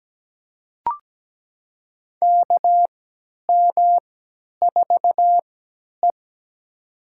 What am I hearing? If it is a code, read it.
Morse code, heard as a keyed tone, about 17 wpm, spaced as for 10 wpm, 700 Hz. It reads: KM4E
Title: USA Call Signs - 4 Characters - 3x Character Spacing - 17wpm